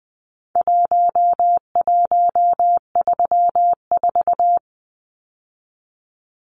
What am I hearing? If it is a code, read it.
Morse code, heard as a keyed tone, about 20 wpm, 700 Hz. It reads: 1134
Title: Numbers - Contest Style - 4-Digit 20wpm